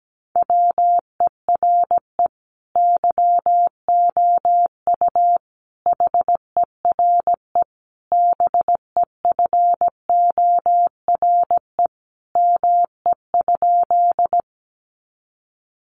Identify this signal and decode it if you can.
Morse code, heard as a keyed tone, about 17 wpm, 700 Hz. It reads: WERE YOU HERE BEFORE ME?
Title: Sentences from Top 200 Words 17wpm